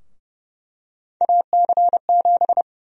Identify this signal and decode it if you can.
Morse code, heard as a keyed tone, about 30 wpm, 700 Hz. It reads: AC7
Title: USA Call Sign Prefixes - Repeated 2x - 30wpm